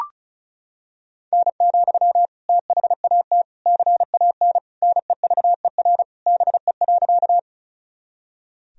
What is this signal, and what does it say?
Morse code, heard as a keyed tone, about 36 wpm, 700 Hz. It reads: N, THAT CAN NEVER BE.
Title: Sentences from Top 300 Words and Abbreviations 35wpm